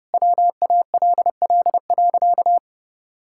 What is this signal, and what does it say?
Morse code, heard as a keyed tone, about 30 wpm, 700 Hz. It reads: WALL.
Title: Sentences from Top 1600 Words 30wpm